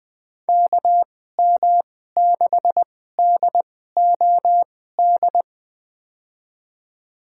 Morse code, heard as a keyed tone, about 20 wpm, spaced as for 14 wpm, 700 Hz. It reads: KM6DOD
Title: USA Call Signs - 6 Characters - 2x Character Spacing - 20wpm